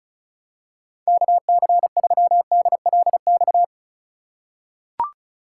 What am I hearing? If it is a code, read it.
Morse code, heard as a keyed tone, about 36 wpm, 700 Hz. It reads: KC3DLX
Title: USA Call Signs - 6 Characters - 2x Character Spacing - 35wpm